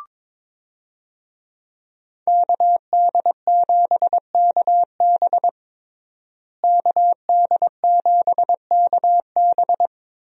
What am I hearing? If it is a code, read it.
Morse code, heard as a keyed tone, about 22 wpm, 700 Hz. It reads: KD7KB KD7KB
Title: USA Call Signs - 5 Characters - Repeated 2x - 22wpm